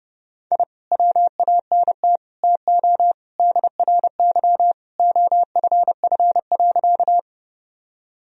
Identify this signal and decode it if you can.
Morse code, heard as a keyed tone, about 30 wpm, 700 Hz. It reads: I WANT TO DRY OFF.
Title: Sentences from Top 1000 Words - Review Part 3 - 30wpm